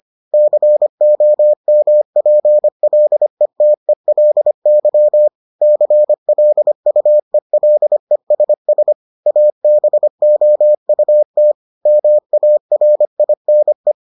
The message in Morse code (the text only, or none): COMPLETELY CLUELESS ABOUT MARINE